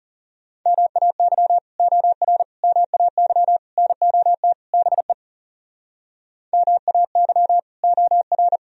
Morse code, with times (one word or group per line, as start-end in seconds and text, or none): 0.7-1.6 MAY
1.8-2.4 OR
2.6-3.6 MAY
3.8-4.5 NOT
4.7-5.1 BE
6.5-7.6 MAY
7.8-8.6 OR